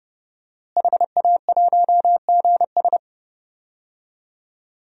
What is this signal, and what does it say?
Morse code, heard as a keyed tone, about 32 wpm, 700 Hz. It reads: HA1GS
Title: DXCC-100 Call Signs - 5 Characters - Repeated 2x - 30wpm